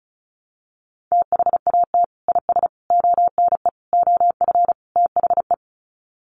Morse code, heard as a keyed tone, about 34 wpm, 700 Hz. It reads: THAT IS ONE OF THE
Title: Sets of 5 Words Encore 35wpm